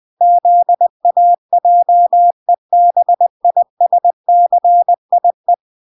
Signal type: Morse code